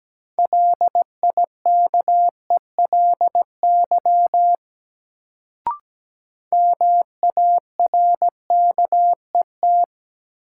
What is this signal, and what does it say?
Morse code, heard as a keyed tone, about 17 wpm, 700 Hz. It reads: LIKELY MARKET
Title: Sentences from Top 600 Words - New Word Drill - 17wpm